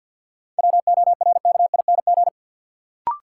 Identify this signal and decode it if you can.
Morse code, heard as a keyed tone, about 51 wpm, 700 Hz. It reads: WORKING
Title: Sentences from Top 700 Words - New Word Drill - 50wpm